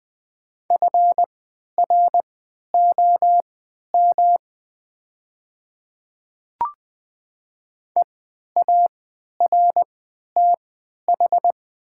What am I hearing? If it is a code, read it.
Morse code, heard as a keyed tone, about 20 wpm, spaced as for 11 wpm, 700 Hz. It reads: FROM EARTH
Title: 3x Character Spacing - Top 300 Words - 20wpm